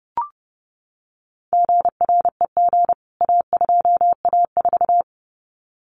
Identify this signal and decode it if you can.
Morse code, heard as a keyed tone, about 30 wpm, 700 Hz. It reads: GREG A2A4
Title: CWT Exchange 30wpm